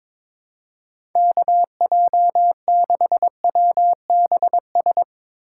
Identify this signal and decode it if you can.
Morse code, heard as a keyed tone, about 22 wpm, 700 Hz. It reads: KJ6WBS
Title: USA Call Signs - 6 Characters - 3x Character Spacing - 22wpm